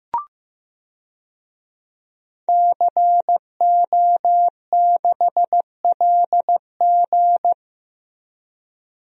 Morse code, heard as a keyed tone, about 15 wpm, 700 Hz. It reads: CO6LG
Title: DXCC-100 Call Signs - 5 Characters - Repeated 3x - 15wpm